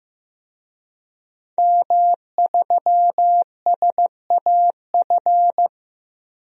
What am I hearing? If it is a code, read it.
Morse code, heard as a keyed tone, about 15 wpm, 700 Hz. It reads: M3SAF
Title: DXCC-100 Call Signs - 5 Characters - Repeated 3x - 15wpm